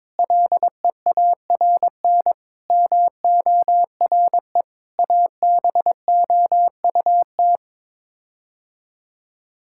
Morse code, heard as a keyed tone, about 22 wpm, 700 Hz. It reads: LEARN MORE ABOUT
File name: practice / 22wpm / Sets of 7 Words - 22wpm